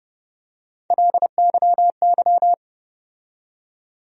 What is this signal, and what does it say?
Morse code, heard as a keyed tone, about 30 wpm, 700 Hz. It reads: LYY
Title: USA Call Sign Postfixes - 3 Letters 30wpm